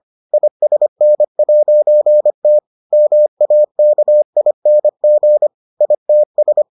Morse code, heard as a keyed tone, about 25 wpm, 600 Hz. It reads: ISN'T MAKING ITS